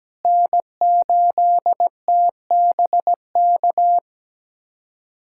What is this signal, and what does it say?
Morse code, heard as a keyed tone, about 17 wpm, 700 Hz. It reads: N8TBK
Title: USA Callsign DE Callsign - 5 Characters - 17wpm